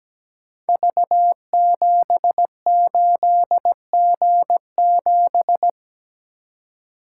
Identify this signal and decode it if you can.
Morse code, heard as a keyed tone, about 17 wpm, 700 Hz. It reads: V78G7
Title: Code Groups - 5 Letter:Numbers - 17wpm